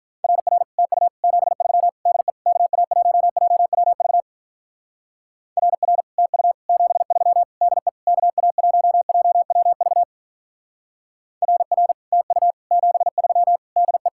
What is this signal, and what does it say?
Morse code, heard as a keyed tone, about 44 wpm, 700 Hz. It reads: RR TU 73 DE KA1JWV RR TU 73 DE KA1JWV RR TU 73 DE
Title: Speed-Racing Common QSO Phrases Encore - 35wpm